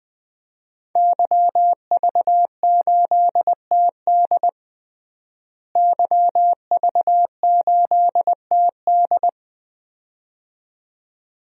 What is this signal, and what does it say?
Morse code, heard as a keyed tone, about 20 wpm, 700 Hz. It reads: YV8TD YV8TD
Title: DXCC-100 Call Signs - 5 Characters - Repeated 2x - 20wpm